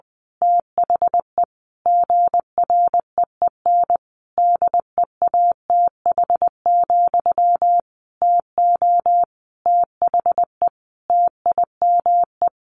Morse code, heard as a keyed tone, about 20 wpm, 700 Hz. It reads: THE GREEN DEATH, TO THE TIME
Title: A Sound Of Thunder 20wpm